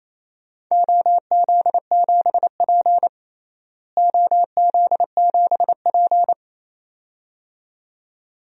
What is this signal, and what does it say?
Morse code, heard as a keyed tone, about 28 wpm, 700 Hz. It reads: OZ7P OZ7P
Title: DXCC-100 Call Signs - 4 Characters - Repeated 2x - 28wpm